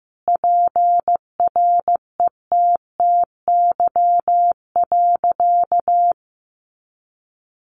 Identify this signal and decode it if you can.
Morse code, heard as a keyed tone, about 15 wpm, 700 Hz. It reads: PRETTY.
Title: Sentences from Top 1000 Words 15wpm